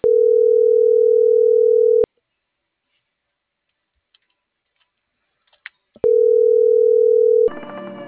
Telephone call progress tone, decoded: ringback tone

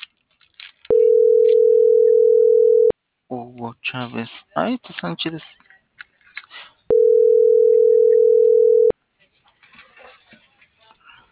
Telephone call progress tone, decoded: ringback tone

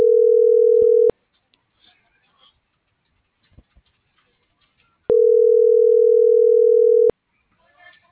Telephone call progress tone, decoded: ringback tone

